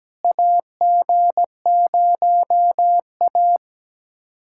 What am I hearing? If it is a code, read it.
Morse code, heard as a keyed tone, about 17 wpm, 700 Hz. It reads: AG0A
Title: USA Callsign DE Callsign - 4 Characters - 17wpm